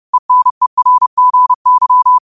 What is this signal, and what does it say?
Morse code, heard as a keyed tone, about 30 wpm, 1000 Hz. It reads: ENERGY
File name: qotdfiles / 30wpm